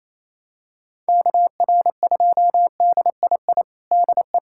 Morse code, heard as a keyed tone, about 28 wpm, 700 Hz. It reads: KR2DII DE
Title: USA Callsign DE Callsign 28wpm